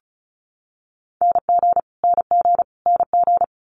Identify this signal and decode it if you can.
Morse code, heard as a keyed tone, about 36 wpm, 700 Hz. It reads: NG NG NG
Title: Multi-letter Phonemes Repeated 3x 35wpm